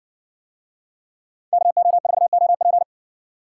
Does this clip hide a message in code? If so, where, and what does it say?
Yes: Morse code from 1.5 to 2.8 s, heard as a keyed tone, about 56 wpm, 700 Hz. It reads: KO4OP